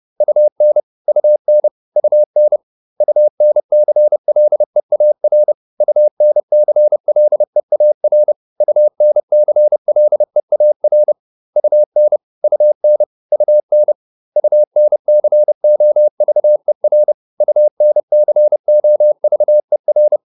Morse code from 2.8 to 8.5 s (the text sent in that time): UNCLEAR UNCLEAR